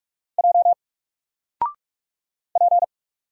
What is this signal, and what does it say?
Morse code, heard as a keyed tone, about 44 wpm, 700 Hz. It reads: J P
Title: Single Character (Letter:Number:Punctuation:Prosign) Alt 45wpm